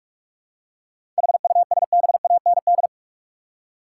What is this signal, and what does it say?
Morse code, heard as a keyed tone, about 45 wpm, 700 Hz. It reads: HUSBAND
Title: Seven Letter Words - 45wpm